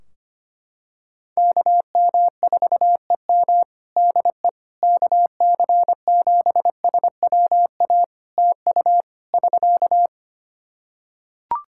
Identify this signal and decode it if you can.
Morse code, heard as a keyed tone, about 25 wpm, 700 Hz. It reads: KM4EM DE KC7SWA TU <SK>
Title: Practice QSO - Exchange 4 - Ending - 25wpm